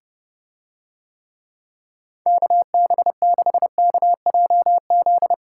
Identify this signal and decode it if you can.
Morse code, heard as a keyed tone, about 30 wpm, 700 Hz. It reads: KB6KJZ